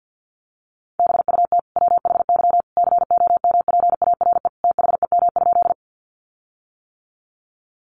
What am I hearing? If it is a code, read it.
Morse code, heard as a keyed tone, about 50 wpm, 700 Hz. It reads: BUT WHY COMPARE THEM?
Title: Sentences from Top 1300 Words 50wpm